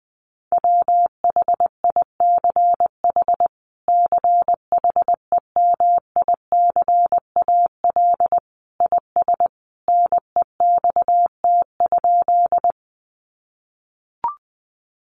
Morse code, heard as a keyed tone, about 20 wpm, 700 Hz. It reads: WHICH CHEMICAL IS NEXT?
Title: Sentences from Top 1200 Words 20wpm